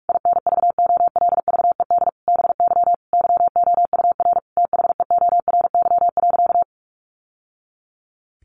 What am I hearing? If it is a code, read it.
Morse code, heard as a keyed tone, about 44 wpm, 700 Hz. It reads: INVOLVED BY YOUR THEORY.
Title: Sentences from Top 800 Words 45wpm